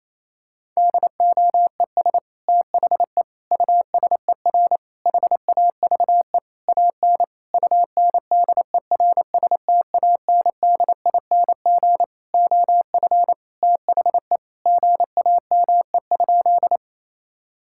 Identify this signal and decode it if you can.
Morse code, heard as a keyed tone, about 28 wpm, 700 Hz. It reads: DOES THE USER HAVE AN UNDERSTANDING OF THE GAME?
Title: Sentences from Top 1000 Words - Review Part 5 - 28wpm